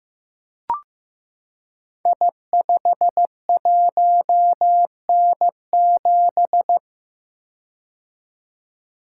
Morse code, heard as a keyed tone, about 15 wpm, 700 Hz. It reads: I51N7
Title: Code Groups - 5 Letter:Numbers - 15wpm